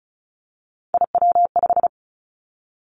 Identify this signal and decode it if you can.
Morse code, heard as a keyed tone, about 36 wpm, 700 Hz. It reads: IW5
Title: Three Challenging Letter-Number Combinations 35wpm